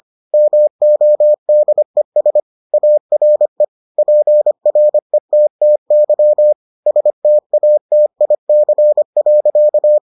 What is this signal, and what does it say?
Morse code, heard as a keyed tone, about 25 wpm, 600 Hz. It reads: MODES ARE PRETTY STATIC.